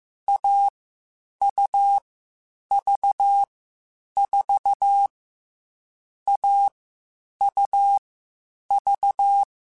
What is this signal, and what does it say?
Morse code, heard as a keyed tone, about 15 wpm, 800 Hz. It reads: A U V 4 A U V
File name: Short Runs / Short Runs-mp3s / AUV AUV4 @13